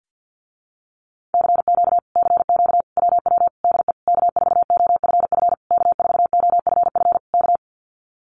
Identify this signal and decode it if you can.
Morse code, heard as a keyed tone, about 50 wpm, 700 Hz. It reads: CQ CQ WW DE K4OFF K4OFF K
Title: Common QSO Phrases Encore 50wpm